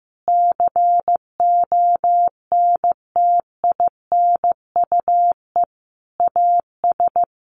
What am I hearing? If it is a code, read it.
Morse code, heard as a keyed tone, about 15 wpm, 700 Hz. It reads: CONTINUE AS
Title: Sentences from Top 1400 Words 15wpm